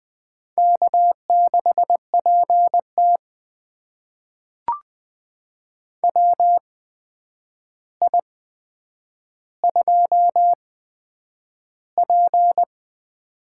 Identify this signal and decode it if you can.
Morse code, heard as a keyed tone, about 20 wpm, spaced as for 9 wpm, 700 Hz. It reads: K6PT WI2P
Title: USA Call Signs - 4 Characters - 8x Character Spacing - 20wpm